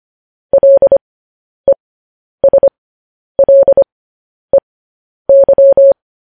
Morse code, heard as a keyed tone, about 25 wpm, spaced as for 10 wpm, 550 Hz. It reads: LESLEY
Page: news headlines in morse code at 10 wpm